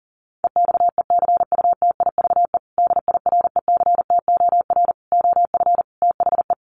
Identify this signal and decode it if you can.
Morse code, heard as a keyed tone, about 40 wpm, 700 Hz. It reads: EXECUTIVE DIRECTOR OF THE
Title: Sets of 5 Words Encore2 40wpm